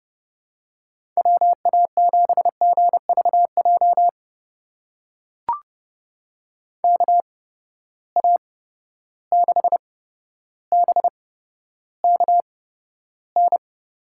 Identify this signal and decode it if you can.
Morse code, heard as a keyed tone, about 30 wpm, spaced as for 13 wpm, 700 Hz. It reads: WA7GVJ KA6BKN
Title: USA Call Signs - 6 Characters - 8x Character Spacing - 30wpm